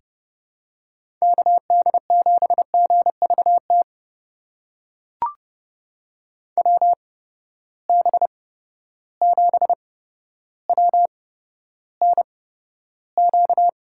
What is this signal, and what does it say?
Morse code, heard as a keyed tone, about 32 wpm, spaced as for 13 wpm, 700 Hz. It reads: KD7GVT WB7WNQ
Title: USA Call Signs - 6 Characters - 8x Character Spacing - 30wpm